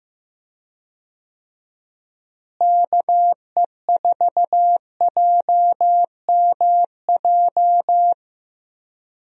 Morse code, heard as a keyed tone, about 16 wpm, 700 Hz. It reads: KE4JMJ
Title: USA Call Signs - 6 Characters - Repeated 2x - 15wpm